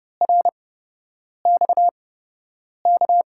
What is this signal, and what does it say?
Morse code, heard as a keyed tone, about 30 wpm, spaced as for 8 wpm, 700 Hz. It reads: RXK
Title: USA Call Signs - 6 Characters - 8x Character Spacing - 30wpm